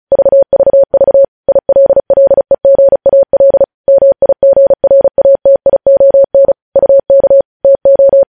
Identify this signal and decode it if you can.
Morse code, heard as a keyed tone, about 36 wpm, 550 Hz. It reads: VVV ILLEGAL MIGRATION UK TO
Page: News headlines in Morse code at 35 wpm